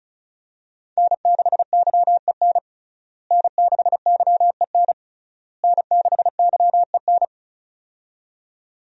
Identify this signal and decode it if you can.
Morse code, heard as a keyed tone, about 34 wpm, 700 Hz. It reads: N6YEN N6YEN N6YEN
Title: USA Call Signs - 5 Characters - Repeated 3x - 35wpm